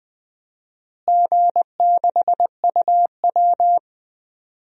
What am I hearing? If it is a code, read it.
Morse code, heard as a keyed tone, about 21 wpm, 700 Hz. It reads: G6UW
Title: DXCC-100 Call Signs - 4 Characters - Repeated 2x - 20wpm